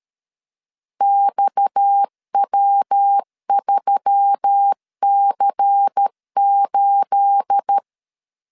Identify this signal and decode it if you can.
Morse code, heard as a keyed tone, about 13 wpm, 800 Hz. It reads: XW3C8
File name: CW / DET / 1 9 1500C S C 12wpm 3811